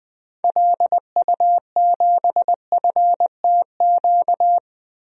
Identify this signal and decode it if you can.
Morse code, heard as a keyed tone, about 20 wpm, 700 Hz. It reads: LU7FTQ
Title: DXCC-100 Call Signs - 6 Characters - 20wpm